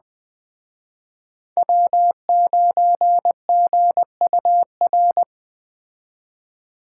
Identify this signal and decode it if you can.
Morse code, heard as a keyed tone, about 21 wpm, 700 Hz. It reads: W9GUR